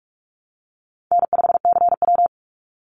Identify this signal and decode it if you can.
Morse code, heard as a keyed tone, about 44 wpm, 700 Hz. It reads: N5CW
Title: Easy Call Signs Repeated 2x - 45wpm